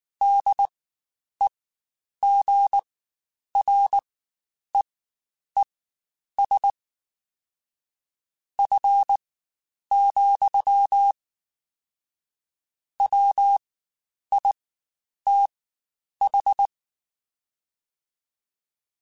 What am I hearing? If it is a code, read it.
Morse code, heard as a keyed tone, about 19 wpm, spaced as for 9 wpm, 800 Hz. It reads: DEGREES F, WITH
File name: Spaced Characters / Spaced Characters-mp3s / Weather 3 17 sp @18